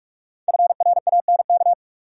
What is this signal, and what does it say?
Morse code, heard as a keyed tone, about 43 wpm, 700 Hz. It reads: FRANK
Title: USA Common Names - Max 5 Letters - 45wpm